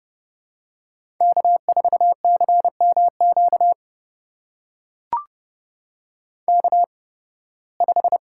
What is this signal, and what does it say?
Morse code, heard as a keyed tone, about 30 wpm, spaced as for 20 wpm, 700 Hz. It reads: K4CMQ K5
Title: USA Call Signs - 5 Characters - 8x Character Spacing - 30wpm